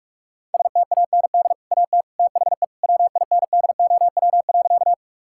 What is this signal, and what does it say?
Morse code, heard as a keyed tone, about 41 wpm, 700 Hz. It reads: STAND AT THE WINDOW.
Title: Sentences from Top 1600 Words 45wpm